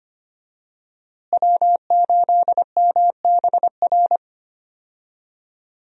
Morse code, heard as a keyed tone, about 25 wpm, 700 Hz. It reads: W8MBR
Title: USA Call Signs - 5 Characters - Repeated 2x - 25wpm